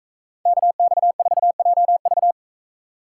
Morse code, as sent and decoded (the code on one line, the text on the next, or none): -.- -..- ....- .--- ...-
KX4JV